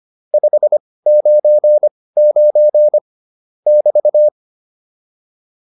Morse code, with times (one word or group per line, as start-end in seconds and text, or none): 0.3-3.0 599
3.7-4.3 <BT>